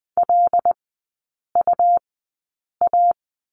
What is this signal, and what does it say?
Morse code, heard as a keyed tone, about 20 wpm, spaced as for 8 wpm, 690 Hz. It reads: LUA
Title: 159 Lesson 26e - Letter g - Three Characters 20wpm